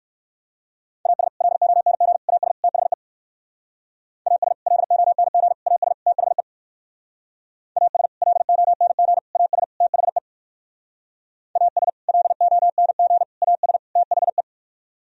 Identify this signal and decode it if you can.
Morse code, heard as a keyed tone, about 66 wpm, 700 Hz. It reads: AS LONG AS THE AS LONG AS THE AS LONG AS THE AS LONG AS THE